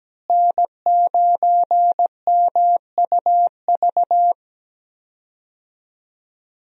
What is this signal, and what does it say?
Morse code, heard as a keyed tone, about 17 wpm, 700 Hz. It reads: N9MUV
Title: USA Call Signs - 5 Characters - Repeated 2x - 17wpm